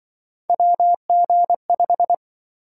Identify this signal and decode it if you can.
Morse code, heard as a keyed tone, about 24 wpm, 700 Hz. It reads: WG5
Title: Speed-Racing - Three Letter-Number Combinations - 20wpm